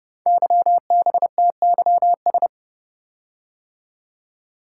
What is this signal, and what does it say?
Morse code, heard as a keyed tone, about 30 wpm, 700 Hz. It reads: YBTYS
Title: Code Groups - 5 Letters - 30wpm